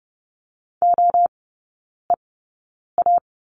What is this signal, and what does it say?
Morse code, heard as a keyed tone, about 30 wpm, spaced as for 9 wpm, 700 Hz. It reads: OEA
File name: Lesson 17e - Letter u - Three Characters 30wpm